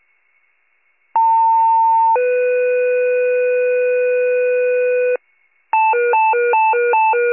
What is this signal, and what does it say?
Two-tone page, heard as a one-second tone, then a three-second tone, about 900 then 500 Hz, with noise. It is heard from 1.2 to 5.2 s.